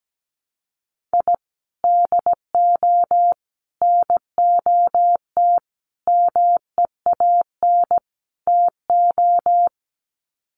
Morse code, heard as a keyed tone, about 17 wpm, 700 Hz. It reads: I DO NOT MEAN TO
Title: Sets of 5 Words 17wpm